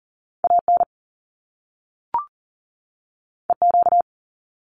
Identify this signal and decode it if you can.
Morse code, heard as a keyed tone, about 41 wpm, 700 Hz. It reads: AN EQ
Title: Top Two Letter Combinations in Words 40wpm